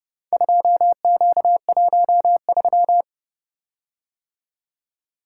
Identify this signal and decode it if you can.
Morse code, heard as a keyed tone, about 30 wpm, 700 Hz. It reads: 2Q13